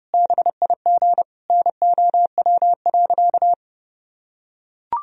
Morse code, heard as a keyed tone, about 30 wpm, 700 Hz. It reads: BIG NOW.
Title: Sentences from Top 1000 Words - Review Part 1 - 30wpm